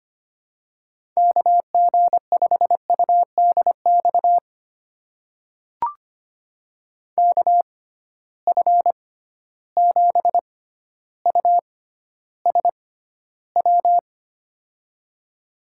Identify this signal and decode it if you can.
Morse code, heard as a keyed tone, about 25 wpm, spaced as for 13 wpm, 700 Hz. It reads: KG5UDX KF7USW
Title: USA Call Signs - 6 Characters - 6x Character Spacing - 25wpm